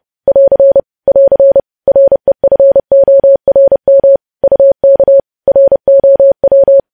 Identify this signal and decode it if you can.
Morse code, heard as a keyed tone, about 30 wpm, 550 Hz. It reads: <AR> <AR> REFORM UK ROW
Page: news headlines in morse code at 30 wpm